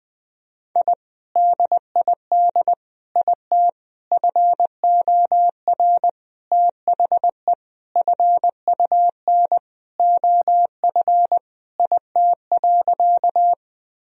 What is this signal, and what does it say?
Morse code, heard as a keyed tone, about 20 wpm, 700 Hz. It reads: I DID IT FOR THE FUN OF IT.
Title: Sentences from Top 1000 Words - Review Part 4 - 20wpm